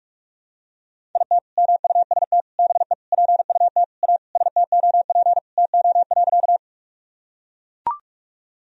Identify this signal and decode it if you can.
Morse code, heard as a keyed tone, about 42 wpm, 700 Hz. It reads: IT MUST BE PUT A STOP TO.
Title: Sentences from Top 1000 Words - Review Part 1 - 45wpm